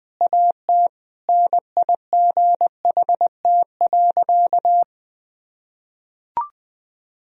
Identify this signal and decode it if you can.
Morse code, heard as a keyed tone, about 20 wpm, 700 Hz. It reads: AT NIGHT.
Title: Sentences from Top 1000 Words - Review Part 2 - 20wpm